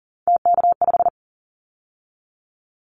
Morse code, heard as a keyed tone, about 42 wpm, 690 Hz. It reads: TK5
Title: Callsign Prefixes for Top 100 DXCC - 40wpm